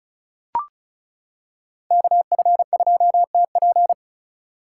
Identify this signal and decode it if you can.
Morse code, heard as a keyed tone, about 36 wpm, 700 Hz. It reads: KF2TP
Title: K1USN SST QSO Practice - 35wpm